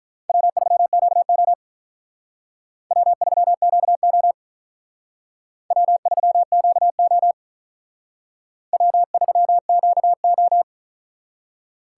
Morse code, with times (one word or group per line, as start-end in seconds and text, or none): 0.3-1.5 W3QO
2.9-4.3 W3QO
5.7-7.3 W3QO
8.7-10.6 W3QO